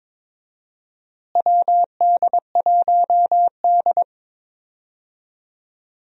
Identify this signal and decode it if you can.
Morse code, heard as a keyed tone, about 22 wpm, 700 Hz. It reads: WD1D
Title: USA Call Signs - 4 Characters - Repeated 3x - 22wpm